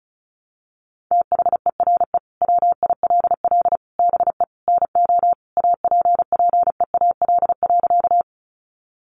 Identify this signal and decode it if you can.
Morse code, heard as a keyed tone, about 34 wpm, 700 Hz. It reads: THERE WILL BE NO APPEAL.
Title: Sentences from Top 1300 Words 35wpm